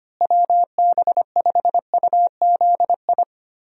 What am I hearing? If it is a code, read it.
Morse code, heard as a keyed tone, about 25 wpm, 700 Hz. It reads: WB5UZI
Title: USA Call Signs - 6 Characters - Repeated 3x - 25wpm